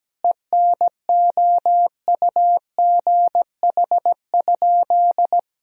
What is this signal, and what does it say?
Morse code, heard as a keyed tone, about 17 wpm, 700 Hz. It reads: ENOUGH?
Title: Sentences from Top 1000 Words - Review Part 3 - 17wpm